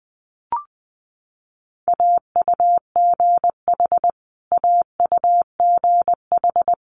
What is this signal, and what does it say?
Morse code, heard as a keyed tone, about 20 wpm, 700 Hz. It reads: AUGH AUGH
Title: Multi-letter Phonemes Repeated 3x 20wpm